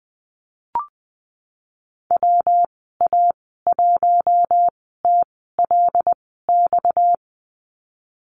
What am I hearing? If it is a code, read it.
Morse code, heard as a keyed tone, about 20 wpm, spaced as for 15 wpm, 700 Hz. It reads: WA1TLX
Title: 245 Lesson 39g - Letter z - Call Signs 20wpm